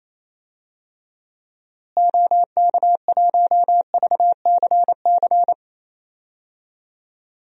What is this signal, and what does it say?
Morse code, heard as a keyed tone, about 28 wpm, 700 Hz. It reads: OK1VCC
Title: DXCC-100 Call Signs - 6 Characters - Repeated 2x - 28wpm